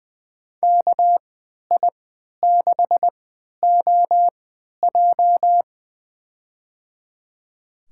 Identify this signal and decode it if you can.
Morse code, heard as a keyed tone, about 20 wpm, spaced as for 11 wpm, 700 Hz. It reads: KI6OJ
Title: USA Call Signs - 5 Characters - 3x Character Spacing - 20wpm